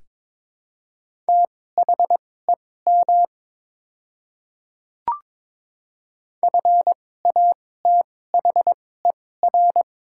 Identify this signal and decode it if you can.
Morse code, heard as a keyed tone, about 21 wpm, spaced as for 16 wpm, 700 Hz. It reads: THEM FATHER
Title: 2x Character Spacing - Top 200 Words - 22wpm